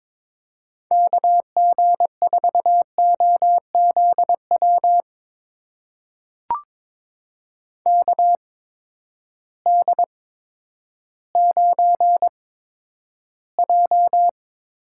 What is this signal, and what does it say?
Morse code, heard as a keyed tone, about 22 wpm, spaced as for 11 wpm, 700 Hz. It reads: KG4OZW KD9J
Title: USA Call Signs - 8x Character Spacing - 22wpm